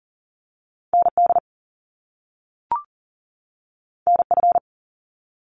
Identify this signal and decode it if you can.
Morse code, heard as a keyed tone, about 40 wpm, 700 Hz. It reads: ND NF
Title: Top Two Letter Combinations in Words 40wpm